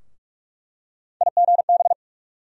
Morse code, heard as a keyed tone, about 47 wpm, 700 Hz. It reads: IGB
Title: USA Call Sign Postfixes - 3 Letters 45wpm